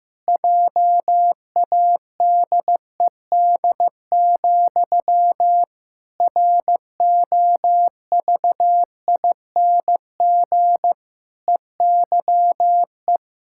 Morse code, heard as a keyed tone, about 14 wpm, 700 Hz. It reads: JADED, ROVING EYE
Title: Pangram Challenge v2 - 15wpm